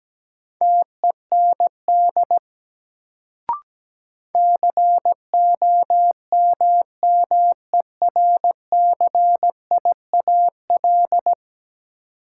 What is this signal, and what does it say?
Morse code, heard as a keyed tone, about 17 wpm, 700 Hz. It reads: TEND COMMERCIAL
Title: Sentences from Top 1000 Words - Review Part 5 - New Word Drill - 17wpm